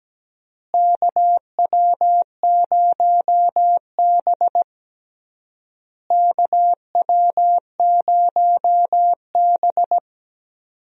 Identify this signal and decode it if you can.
Morse code, heard as a keyed tone, about 17 wpm, 700 Hz. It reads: KW0B KW0B
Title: USA Call Signs - 4 Characters - Repeated 3x - 17wpm